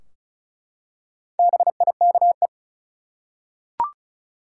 Morse code, heard as a keyed tone, about 36 wpm, 700 Hz. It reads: BIKE
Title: Four Letter Words - 35wpm